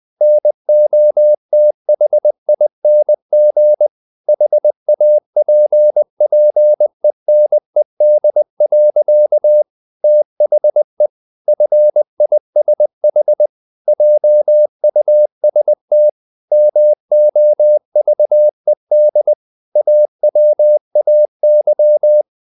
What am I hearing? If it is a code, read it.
Morse code, heard as a keyed tone, about 20 wpm, 600 Hz. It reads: NOTHING HAPPENED. THE FISH JUST MOVED AWAY
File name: the old man and the sea / the old man and the sea part 1 20wpm